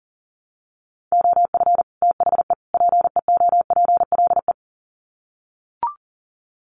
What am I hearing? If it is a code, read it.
Morse code, heard as a keyed tone, about 40 wpm, 700 Hz. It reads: OF THE PEOPLE
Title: Sets of 3 Words 40wpm